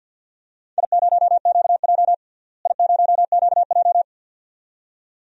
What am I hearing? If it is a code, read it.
Morse code, heard as a keyed tone, about 48 wpm, 700 Hz. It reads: I0QJ I0QJ
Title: DXCC-100 Call Signs - 4 Characters - Repeated 2x - 50wpm